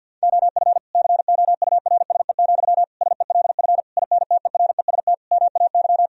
Morse code, heard as a keyed tone, about 50 wpm, 700 Hz. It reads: OF COURSE, SELF INTEREST MAY
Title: Sentences from Top 1000 Words - Review Part 2 - 50wpm